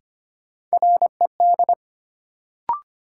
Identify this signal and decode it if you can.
Morse code, heard as a keyed tone, about 25 wpm, 700 Hz. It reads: RED
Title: Sentences from Top 700 Words - New Word Drill - 25wpm